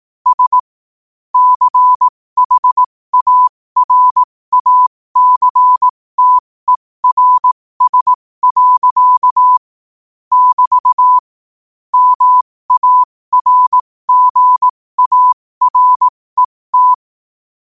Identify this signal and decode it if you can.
Morse code, heard as a keyed tone, about 18 wpm, spaced as for 15 wpm, 1000 Hz. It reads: S CHARACTERS. <BT> MARGARET
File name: qotdfiles / 15wpm